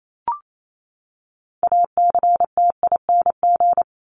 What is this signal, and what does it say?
Morse code, heard as a keyed tone, about 28 wpm, 700 Hz. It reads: ACTING
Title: ING Suffix - Top 100 Words with Base Words 28wpm